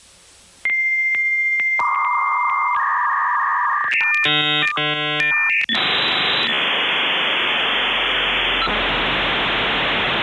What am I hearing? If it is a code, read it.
Dial-up modem handshake.